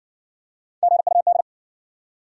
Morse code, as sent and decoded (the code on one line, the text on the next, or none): --. ..- -..
GUD